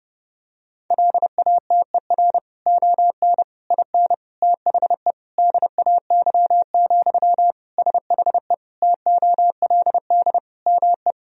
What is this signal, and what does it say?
Morse code, heard as a keyed tone, about 30 wpm, 700 Hz. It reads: LATER ON IN THE DAY, SHE TOLD ME
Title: Sentences from Top 1000 Words - Review Part 4 - 30wpm